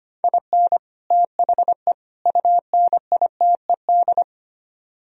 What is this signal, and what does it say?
Morse code, heard as a keyed tone, about 31 wpm, 700 Hz. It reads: IN THE UNITED
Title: Speed-Racing - Sets of 3 Words 25wpm